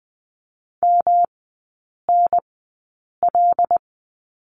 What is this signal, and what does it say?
Morse code, heard as a keyed tone, about 19 wpm, spaced as for 8 wpm, 700 Hz. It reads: MNL
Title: 133 Lesson 22e - Question Mark - Three Characters 20wpm